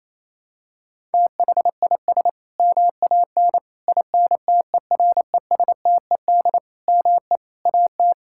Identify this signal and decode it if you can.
Morse code, heard as a keyed tone, about 27 wpm, 700 Hz. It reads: THIS MAN INTERESTED ME AT
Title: Sentences from Top 1000 Words - Review Part 6 - 28wpm